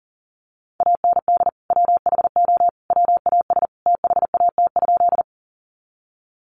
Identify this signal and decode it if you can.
Morse code, heard as a keyed tone, about 40 wpm, 700 Hz. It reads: AND WHO WAS THAT?
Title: Sentences from Top 100 Words 40wpm